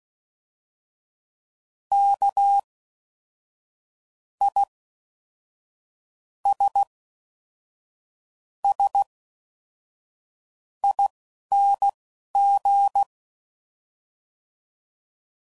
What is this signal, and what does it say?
Morse code, heard as a keyed tone, about 16 wpm, spaced as for 11 wpm, 800 Hz. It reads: K I S S ING
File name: Suffixes / Special INGs / Fast INGs